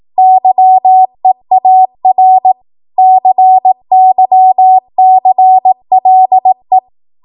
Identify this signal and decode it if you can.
Morse code, heard as a keyed tone, about 18 wpm, 750 Hz. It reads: YEAR CYCLE
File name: Morse / 18WPM